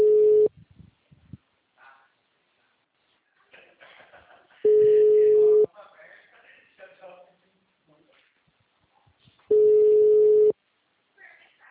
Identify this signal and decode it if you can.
Ringback tone (Europe, 425 Hz), with speech.